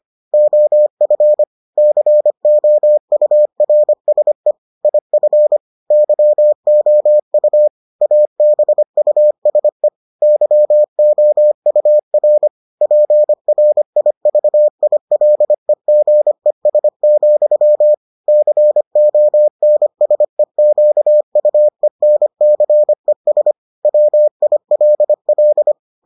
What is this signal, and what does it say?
Morse code, heard as a keyed tone, about 25 wpm, 600 Hz. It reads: OF COURSE IF YOU ABUSE YOUR PRIVILEGES, CONSEQUENCES WILL